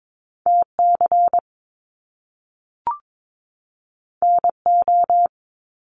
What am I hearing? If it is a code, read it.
Morse code, heard as a keyed tone, about 22 wpm, 700 Hz. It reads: TC NO
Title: Top Two Letter Combinations in Words 22wpm